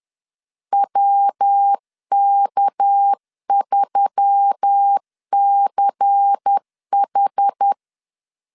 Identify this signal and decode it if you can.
Morse code, heard as a keyed tone, about 10 wpm, 800 Hz. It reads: WK3CH
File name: LEZIONI / 6 PKX0 / 6 PK 10wpm 2436